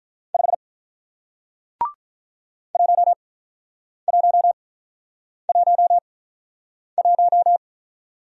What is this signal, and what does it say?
Morse code, heard as a keyed tone, about 48 wpm, spaced as for 9 wpm, 700 Hz. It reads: 5 1111